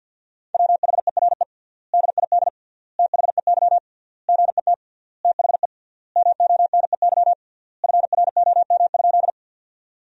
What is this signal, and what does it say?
Morse code, heard as a keyed tone, about 52 wpm, 700 Hz. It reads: WHERE DID THEY GET THE MONEY FROM?
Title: 3x Word-Spacing - Sentences from Top 400 Words - 50wpm